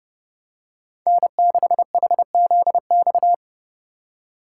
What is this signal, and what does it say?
Morse code, heard as a keyed tone, about 28 wpm, 700 Hz. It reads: N6HZX